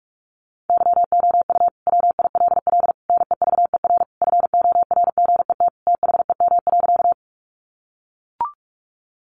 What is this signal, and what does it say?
Morse code, heard as a keyed tone, about 44 wpm, 700 Hz. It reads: YOU WILL NEVER FORGET THEM.
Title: Sentences from Top 1200 Words 45wpm